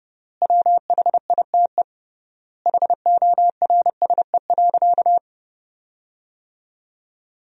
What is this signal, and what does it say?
Morse code, heard as a keyed tone, about 30 wpm, 700 Hz. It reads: WHITE HORSE.
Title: 3x Word-Spacing - Sentences from Top 300 Words - 30wpm